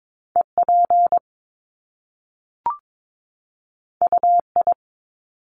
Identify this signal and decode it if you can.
Morse code, heard as a keyed tone, about 22 wpm, 700 Hz. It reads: EP UI
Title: Top Two Letter Combinations in Words 22wpm